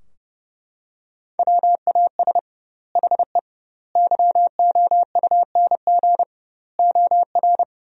Morse code, heard as a keyed tone, about 30 wpm, 700 Hz. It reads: WAS HE YOUNG OR
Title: 2x Word-Spacing - Sentences from Top 400 Words - 30wpm